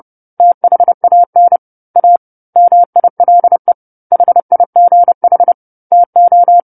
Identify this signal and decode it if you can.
Morse code, heard as a keyed tone, about 30 wpm, 700 Hz. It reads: THAN A MILE HIGH TO
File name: 10-Chapter-10 30 WPM